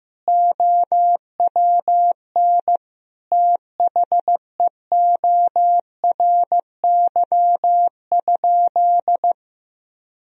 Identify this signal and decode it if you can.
Morse code, heard as a keyed tone, about 16 wpm, 700 Hz. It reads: OWN THEORY?
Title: Sentences from Top 1000 Words - Review Part 3 - 15wpm